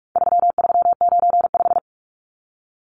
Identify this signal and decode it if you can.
Morse code, heard as a keyed tone, about 44 wpm, 700 Hz. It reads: 3395
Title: Numbers - 4-Digit 45wpm